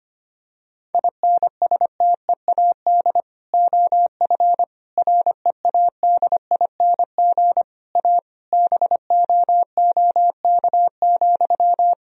Morse code, heard as a keyed tone, about 25 wpm, 690 Hz. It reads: INSTEAD OF READING A BOOK,
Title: Sentences from Top 1000 Words - Review Part 4 - 25wpm